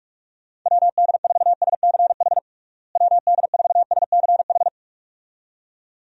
Morse code, heard as a keyed tone, about 46 wpm, 700 Hz. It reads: WD4SCH WD4SCH
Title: USA Call Signs - 6 Characters - Repeated 2x - 45wpm